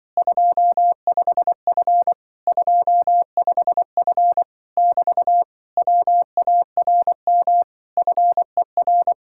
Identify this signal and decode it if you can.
Morse code, heard as a keyed tone, about 24 wpm, 700 Hz. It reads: 25F 25F <BT> WARM FER
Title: Speed-Racing Common QSO Phrases - 20wpm